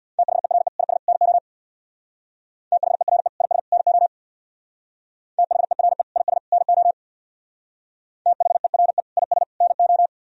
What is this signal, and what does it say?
Morse code, heard as a keyed tone, about 64 wpm, 700 Hz. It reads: THERE IS NO THERE IS NO THERE IS NO THERE IS NO